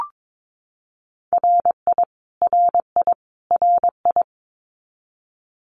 Morse code, heard as a keyed tone, about 22 wpm, 700 Hz. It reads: RI RI RI
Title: US States Repeated 3x 22wpm